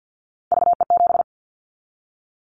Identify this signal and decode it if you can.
Morse code, heard as a keyed tone, about 50 wpm, 700 Hz. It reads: VE7
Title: Callsign Prefixes for Top 100 DXCC - 50wpm